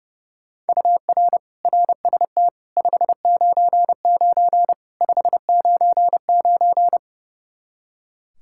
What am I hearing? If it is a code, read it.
Morse code, heard as a keyed tone, about 30 wpm, 700 Hz. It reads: UR RST 599 599
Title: Practice QSO - Exchange 1 - Beginning - 30wpm